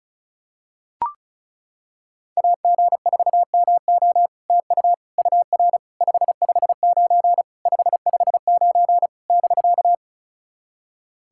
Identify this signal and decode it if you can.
Morse code, heard as a keyed tone, about 36 wpm, 700 Hz. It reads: AG4MO TU UR 559 559 <BK>
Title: SOTA QSO Practice 35wpm